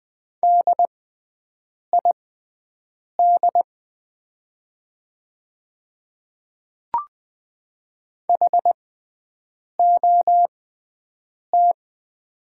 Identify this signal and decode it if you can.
Morse code, heard as a keyed tone, about 20 wpm, spaced as for 7 wpm, 700 Hz. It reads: DID HOT
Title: 6x Character Spacing - Top 100 Words - 20wpm